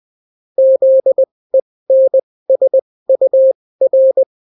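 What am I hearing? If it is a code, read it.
Morse code, heard as a keyed tone, about 20 wpm, spaced as for 16 wpm, 530 Hz. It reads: ZENSUR
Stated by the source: Daily Morse weather and top stories at 16 wpm, clean without QRN.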